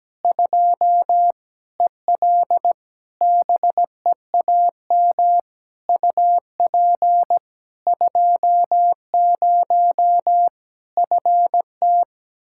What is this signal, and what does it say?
Morse code, heard as a keyed tone, about 17 wpm, 700 Hz. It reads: 2 EL BEAM UP 20 FT
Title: Practice QSO - Exchange 2 - Info - 17wpm